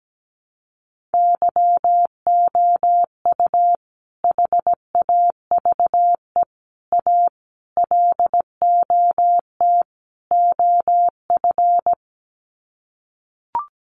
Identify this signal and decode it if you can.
Morse code, heard as a keyed tone, about 17 wpm, 700 Hz. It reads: YOU HAVE A LOT OF
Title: Sets of 5 Words Encore 17wpm